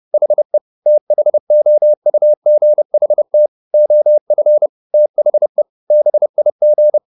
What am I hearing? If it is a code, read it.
Morse code, heard as a keyed tone, about 30 wpm, 600 Hz. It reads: HE THOUGHT OF THE BIG
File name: the old man and the sea / the old man and the sea 1 30wpm